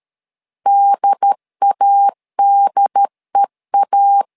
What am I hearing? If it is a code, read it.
Morse code, heard as a keyed tone, about 13 wpm, 800 Hz. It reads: DADEA
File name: LEZIONI / 2 UHDN / 2 UHD 12wpm 1540